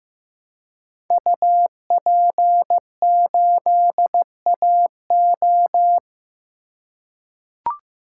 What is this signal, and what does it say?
Morse code, heard as a keyed tone, about 15 wpm, 700 Hz. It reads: UP8AO
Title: Code Groups - 5 Letter:Numbers - 15wpm